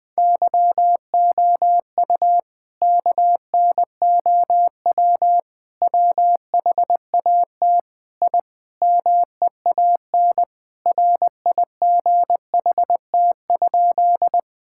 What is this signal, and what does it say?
Morse code, heard as a keyed tone, about 21 wpm, 700 Hz. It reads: YOU KNOW WHAT I MEAN RIGHT?
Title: Sentences with 5-Word Phrases - Spoken Beginning - 20wpm